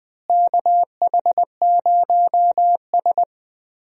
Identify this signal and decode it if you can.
Morse code, heard as a keyed tone, about 20 wpm, 700 Hz. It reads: KH0S
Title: USA Call Signs - 4 Characters - Repeated 3x - 20wpm